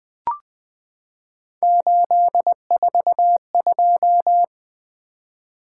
Morse code, heard as a keyed tone, about 20 wpm, 700 Hz. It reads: 842
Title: Three Letter-Number Combinations - 20wpm